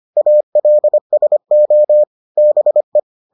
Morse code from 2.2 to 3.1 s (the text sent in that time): BE